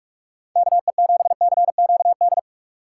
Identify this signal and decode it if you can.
Morse code, heard as a keyed tone, about 47 wpm, 700 Hz. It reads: KE7CQD
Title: USA Call Signs - 6 Characters - 2x Character Spacing - 45wpm